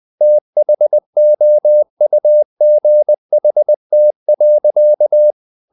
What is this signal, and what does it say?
Morse code, heard as a keyed tone, about 20 wpm, 600 Hz. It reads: THOUGHT.
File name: the old man and the sea / the old man and the sea part 1 20wpm